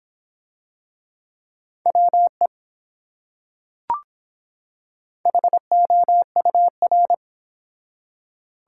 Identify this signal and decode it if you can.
Morse code, heard as a keyed tone, about 24 wpm, 700 Hz. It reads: WE HOUR